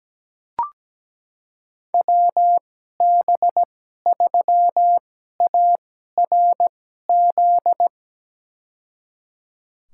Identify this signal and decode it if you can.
Morse code, heard as a keyed tone, about 17 wpm, spaced as for 12 wpm, 700 Hz. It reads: WB3ARZ
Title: USA Call Signs - 6 Characters - 2x Character Spacing - 17wpm